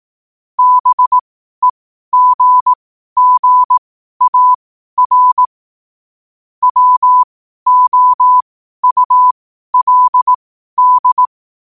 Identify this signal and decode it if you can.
Morse code, heard as a keyed tone, about 18 wpm, spaced as for 12 wpm, 1000 Hz. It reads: BEGGAR WOULD
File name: qotdfiles / 12wpm